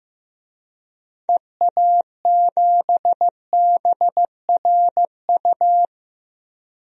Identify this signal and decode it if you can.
Morse code, heard as a keyed tone, about 15 wpm, 700 Hz. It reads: EA7BRU
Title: DXCC-100 Call Signs - 6 Characters - Repeated 3x - 15wpm